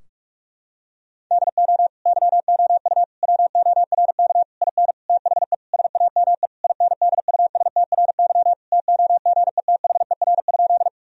Morse code, heard as a keyed tone, about 46 wpm, 700 Hz. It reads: DO YOU WORK IN THE SAME INDUSTRY TOGETHER?
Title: Sentences from Top 1000 Words - Review Part 4 - 45wpm